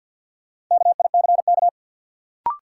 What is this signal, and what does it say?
Morse code, heard as a keyed tone, about 53 wpm, 700 Hz. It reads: KICK
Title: Sentences from Top 1300 Words - New Word Drill Alt - 50wpm